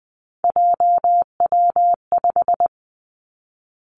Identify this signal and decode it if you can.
Morse code, heard as a keyed tone, about 20 wpm, 700 Hz. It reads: JW5
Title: Three Challenging Letter-Number Combinations 20wpm